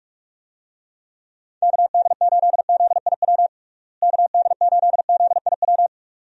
Morse code, heard as a keyed tone, about 46 wpm, 700 Hz. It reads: KD8ZIW KD8ZIW
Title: USA Call Signs - 6 Characters - Repeated 2x - 45wpm